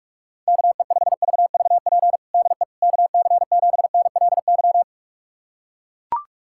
Morse code, heard as a keyed tone, about 42 wpm, 700 Hz. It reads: KE5VVP DE KC7NLY
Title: USA Callsign DE Callsign 45wpm